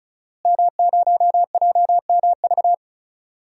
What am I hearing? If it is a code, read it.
Morse code, heard as a keyed tone, about 34 wpm, 700 Hz. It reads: M0JMV
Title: DXCC-100 Call Signs - 5 Characters - Repeated 3x - 35wpm